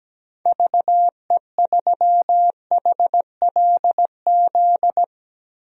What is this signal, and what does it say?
Morse code, heard as a keyed tone, about 17 wpm, 700 Hz. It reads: VE3HLZ
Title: DXCC-100 Call Signs - 6 Characters - 17wpm